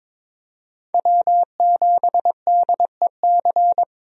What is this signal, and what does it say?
Morse code, heard as a keyed tone, about 22 wpm, 700 Hz. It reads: W7DEC
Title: USA Call Signs - 5 Characters - Repeated 2x - 22wpm